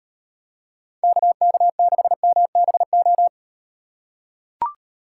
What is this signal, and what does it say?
Morse code, heard as a keyed tone, about 38 wpm, 700 Hz. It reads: KK6MBO